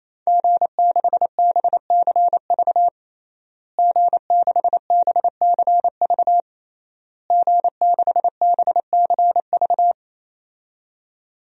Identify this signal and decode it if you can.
Morse code, heard as a keyed tone, about 28 wpm, 690 Hz. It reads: G6BCV G6BCV G6BCV
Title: DXCC-100 Call Signs - 5 Characters - Repeated 3x - 28wpm